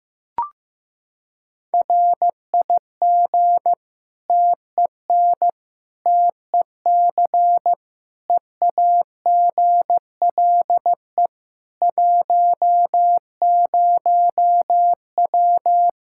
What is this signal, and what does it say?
Morse code, heard as a keyed tone, about 15 wpm, 700 Hz. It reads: RIG TEN TEC EAGLE 10W
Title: Practice QSO - Exchange 2 - Info - 15wpm